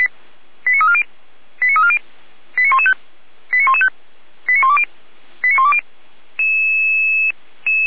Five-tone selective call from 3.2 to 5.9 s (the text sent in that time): 89115 89120 89120